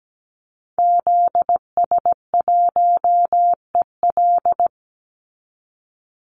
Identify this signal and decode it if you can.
Morse code, heard as a keyed tone, about 17 wpm, 700 Hz. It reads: ZS1EL
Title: Call Signs Repeated 2x - 17wpm